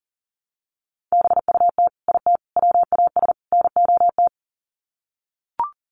Morse code, heard as a keyed tone, about 40 wpm, 700 Hz. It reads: BUT IT WAS NOT
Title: Sets of 4 Words 40wpm